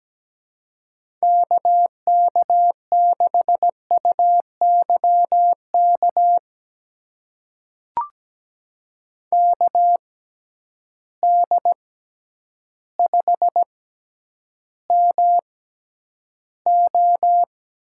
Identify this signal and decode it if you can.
Morse code, heard as a keyed tone, about 17 wpm, spaced as for 9 wpm, 700 Hz. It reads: KK6UYK KD5MO
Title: USA Call Signs - 6 Characters - 6x Character Spacing - 17wpm